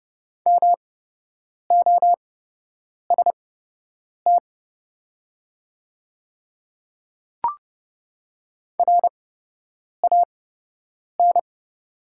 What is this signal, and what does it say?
Morse code, heard as a keyed tone, about 30 wpm, spaced as for 8 wpm, 700 Hz. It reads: MOST RAN
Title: 8x Character Spacing - Top 500 Words - 30wpm